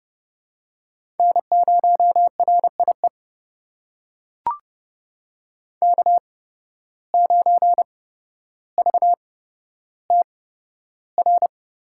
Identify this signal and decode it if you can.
Morse code, heard as a keyed tone, about 32 wpm, spaced as for 13 wpm, 700 Hz. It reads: N0RIE K9VTR
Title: USA Call Signs - 5 Characters - 8x Character Spacing - 30wpm